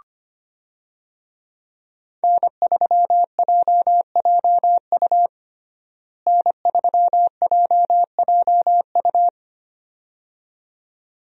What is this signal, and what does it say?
Morse code, heard as a keyed tone, about 25 wpm, 700 Hz. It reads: N3JJU N3JJU
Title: USA Call Signs - 5 Characters - Repeated 2x - 25wpm